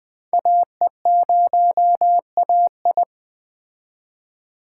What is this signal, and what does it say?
Morse code, heard as a keyed tone, about 21 wpm, 700 Hz. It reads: AE0AI
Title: Speed Racing - USA 2x2 Call Signs 17wpm